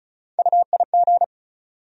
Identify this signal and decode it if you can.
Morse code, heard as a keyed tone, about 34 wpm, 700 Hz. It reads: UIG
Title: USA Call Sign Postfixes - 3 Letters 35wpm